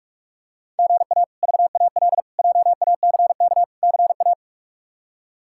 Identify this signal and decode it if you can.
Morse code, heard as a keyed tone, about 43 wpm, 700 Hz. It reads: GA VAL JACK CA
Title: K1USN SST QSO Practice - 45wpm